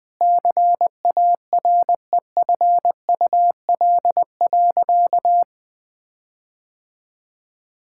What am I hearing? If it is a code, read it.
Morse code, heard as a keyed tone, about 21 wpm, 700 Hz. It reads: CAREFUL.
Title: Sentences from Top 1600 Words 20wpm